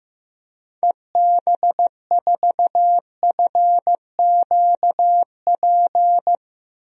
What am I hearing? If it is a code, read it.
Morse code, heard as a keyed tone, about 15 wpm, 700 Hz. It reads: EB4FQP
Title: DXCC-100 Call Signs - 6 Characters - 15wpm